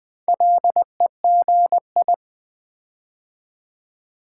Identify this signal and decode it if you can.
Morse code, heard as a keyed tone, about 20 wpm, 700 Hz. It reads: LEGI
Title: Build A Word - 4 Letters At A Time - 11-Letter Words - 20wpm